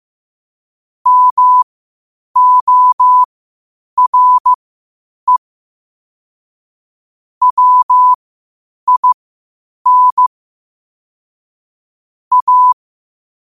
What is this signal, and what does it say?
Morse code, heard as a keyed tone, about 15 wpm, spaced as for 8 wpm, 1000 Hz. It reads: MORE WIN A
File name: 8wpm